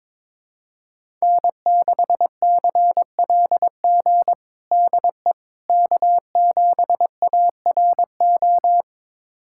Morse code, heard as a keyed tone, about 21 wpm, 700 Hz. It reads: N6CLG DE K7ARO
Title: USA Callsign DE Callsign - 5 Characters - 22wpm